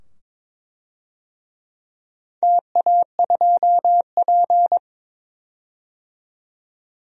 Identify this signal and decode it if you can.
Morse code, heard as a keyed tone, about 22 wpm, 700 Hz. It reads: TA2P
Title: DXCC-100 Call Signs - 4 Characters - 22wpm